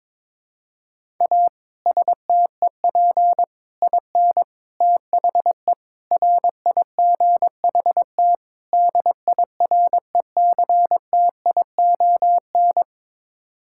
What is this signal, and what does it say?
Morse code, heard as a keyed tone, about 22 wpm, 700 Hz. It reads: A STEP IN THE RIGHT DIRECTION
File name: practice / 22wpm / Sets of 6 Words - 22wpm